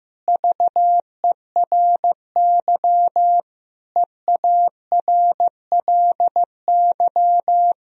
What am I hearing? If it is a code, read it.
Morse code, heard as a keyed tone, about 15 wpm, 700 Hz. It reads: VERY EARLY
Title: Sentences from Top 1000 Words - Review Part 2 - 15wpm